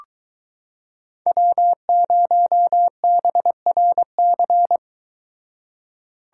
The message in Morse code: W0BRC

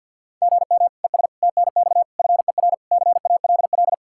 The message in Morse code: GM ES TNX FER CALL

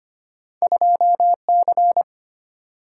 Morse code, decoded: 2C

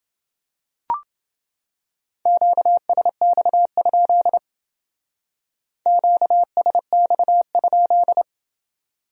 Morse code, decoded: QSX? QSX?

